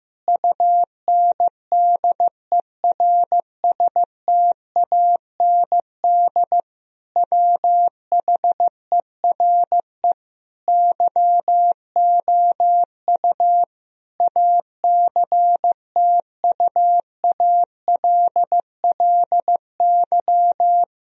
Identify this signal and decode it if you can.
Morse code, heard as a keyed tone, about 15 wpm, 700 Hz. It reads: UNDERSTAND WHERE YOU ACTUALLY
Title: Sentences from Top 1000 Words - Review Part 4 - 15wpm